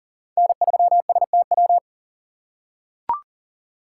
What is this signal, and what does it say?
Morse code, heard as a keyed tone, about 40 wpm, 700 Hz. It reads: N3STW
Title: USA Call Signs - 5 Characters - 3x Character Spacing - 40wpm